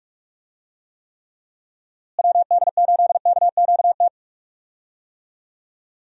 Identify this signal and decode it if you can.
Morse code, heard as a keyed tone, about 44 wpm, 700 Hz. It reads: WD8KQT
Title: USA Call Signs - 6 Characters - Repeated 3x - 45wpm